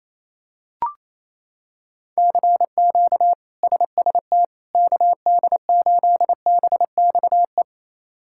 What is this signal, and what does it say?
Morse code, heard as a keyed tone, about 28 wpm, 700 Hz. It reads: CQ SST KD8BXE
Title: K1USN SST QSO Practice - 28wpm